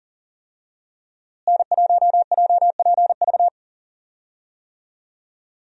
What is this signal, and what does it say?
Morse code, heard as a keyed tone, about 37 wpm, 700 Hz. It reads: N1JPV
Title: USA Call Signs - 5 Characters - Repeated 3x - 40wpm